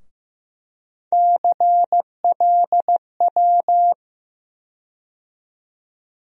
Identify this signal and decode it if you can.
Morse code, heard as a keyed tone, about 15 wpm, 700 Hz. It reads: CLW
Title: USA Call Sign Postfixes - 3 Letters 15wpm